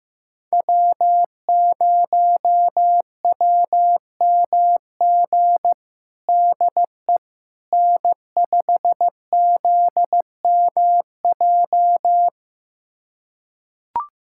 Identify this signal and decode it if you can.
Morse code, heard as a keyed tone, about 15 wpm, 700 Hz. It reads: W0WMG DE N5ZMJ
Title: USA Callsign DE Callsign - 5 Characters - 15wpm